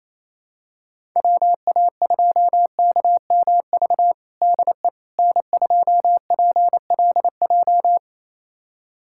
Morse code, heard as a keyed tone, about 28 wpm, 700 Hz. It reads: WA2KMV DE N2PLJ
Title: USA Callsign DE Callsign 28wpm